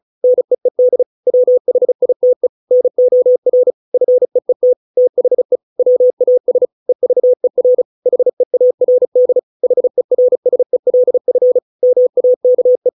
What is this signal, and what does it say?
Morse code, heard as a keyed tone, about 37 wpm, 500 Hz. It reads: NEED WHITE NOR FEET THE WAS EVER HEARD HERSELF MAKE